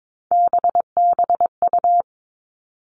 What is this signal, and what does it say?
Morse code, heard as a keyed tone, about 22 wpm, 700 Hz. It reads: BBU
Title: All Cut Numbers - 3-Digit 22wpm